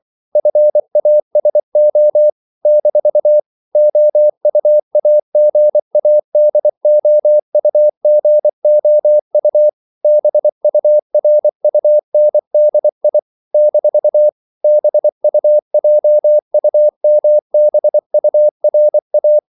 Morse code, heard as a keyed tone, about 24 wpm, 600 Hz. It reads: FASO - OUAGADOUGOU BURUNDI - BUJUMBURA
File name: CW / Practice / NationalCapitals24